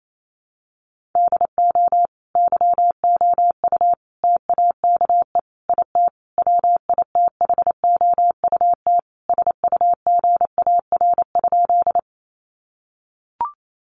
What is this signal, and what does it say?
Morse code, heard as a keyed tone, about 28 wpm, 700 Hz. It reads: DO YOU TAKE IT WITHOUT SUGAR?
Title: Sentences from Top 1200 Words 28wpm